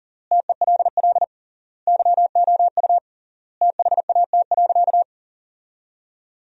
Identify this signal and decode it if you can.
Morse code, heard as a keyed tone, about 40 wpm, 700 Hz. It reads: TELL YOU THAT.
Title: 3x Word-Spacing - Sentences from Top 300 Words - 40wpm